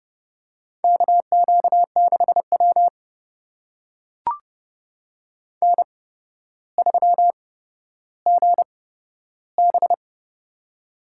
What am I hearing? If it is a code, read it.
Morse code, heard as a keyed tone, about 30 wpm, spaced as for 13 wpm, 700 Hz. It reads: KQ6W N3GB
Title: USA Call Signs - 4 Characters - 8x Character Spacing - 30wpm